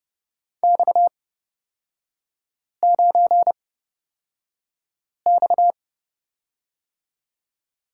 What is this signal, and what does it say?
Morse code, heard as a keyed tone, about 29 wpm, spaced as for 5 wpm, 700 Hz. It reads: X9X
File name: Lesson 37e - Letter x - Three Characters 30wpm